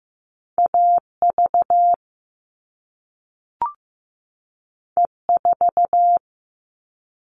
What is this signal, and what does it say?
Morse code, heard as a keyed tone, about 15 wpm, 700 Hz. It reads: AV E4
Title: All Cut Numbers - 2-Digit 15wpm